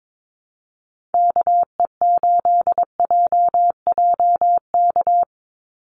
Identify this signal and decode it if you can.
Morse code, heard as a keyed tone, about 22 wpm, 700 Hz. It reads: KE8JJK
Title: USA Call Signs 22wpm